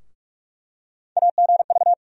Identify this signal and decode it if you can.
Morse code, heard as a keyed tone, about 44 wpm, 700 Hz. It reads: AGV
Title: USA Call Sign Postfixes - 3 Letters 45wpm